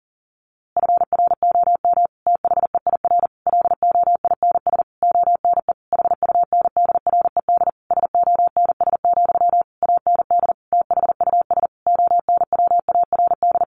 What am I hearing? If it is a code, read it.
Morse code, heard as a keyed tone, about 40 wpm, 700 Hz. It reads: FROM THEIR LOINS ONE HUNDRED SONS, AND THUS ONWARD
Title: A Sound Of Thunder 40wpm